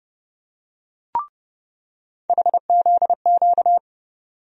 Morse code, heard as a keyed tone, about 30 wpm, 700 Hz. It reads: HZQ
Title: USA Call Sign Postfixes - 3 Letters 30wpm